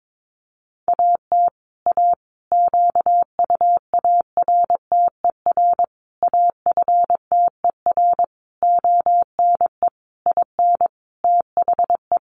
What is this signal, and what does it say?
Morse code, heard as a keyed tone, about 22 wpm, 700 Hz. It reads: AT A QUARTER AFTER ONE IN THE
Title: Sentences from Top 1300 Words 22wpm